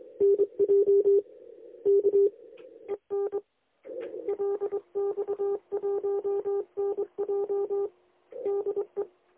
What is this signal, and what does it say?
Morse code, heard as a keyed tone, about 27 wpm, 400 Hz. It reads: NJ K EN LX1NJ DE
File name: lx1nj- -1005utc-10110khz2-49235